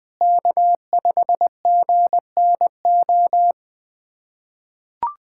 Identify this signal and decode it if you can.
Morse code, heard as a keyed tone, about 21 wpm, 700 Hz. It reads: K5GNO
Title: USA Callsign DE Callsign - 5 Characters - 20wpm